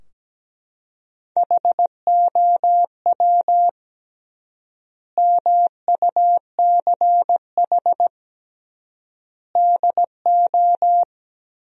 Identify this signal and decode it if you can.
Morse code, heard as a keyed tone, about 17 wpm, 700 Hz. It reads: HOW MUCH DO
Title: 3x Word-Spacing - Sentences from Top 200 Words - 17wpm